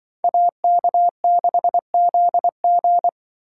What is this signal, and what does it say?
Morse code, heard as a keyed tone, about 24 wpm, 700 Hz. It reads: AK6ZG
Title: Speed Racing - USA 2x2 Call Signs 20wpm